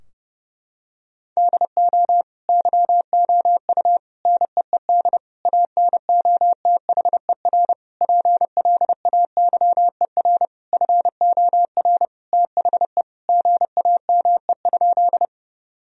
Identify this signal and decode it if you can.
Morse code, heard as a keyed tone, about 32 wpm, 700 Hz. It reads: DO YOU NEED ANOTHER PLAYER FOR THE GAME?
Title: Sentences from Top 1000 Words - Review Part 4 - 30wpm